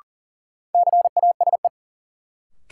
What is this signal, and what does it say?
Morse code, heard as a keyed tone, about 40 wpm, 700 Hz. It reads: CASE
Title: Sentences from Top 700 Words - New Word Drill - 40wpm